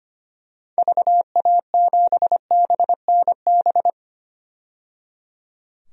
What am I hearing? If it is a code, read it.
Morse code, heard as a keyed tone, about 25 wpm, 700 Hz. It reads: VA7BNB
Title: DXCC-100 Call Signs - 6 Characters - 25wpm